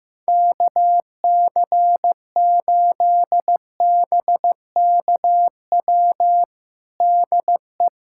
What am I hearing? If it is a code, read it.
Morse code, heard as a keyed tone, about 15 wpm, 700 Hz. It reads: KC8BKW DE
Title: USA Callsign DE Callsign - 6 Characters - 15wpm